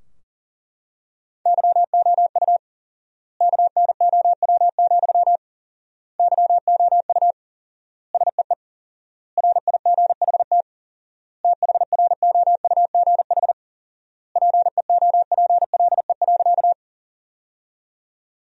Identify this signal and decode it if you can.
Morse code, heard as a keyed tone, about 40 wpm, 700 Hz. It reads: YOU KNOW, YOU SEE RIGHT THROUGH PEOPLE.
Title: 4x Word-Spacing - Sentences from Top 200 Words - 40wpm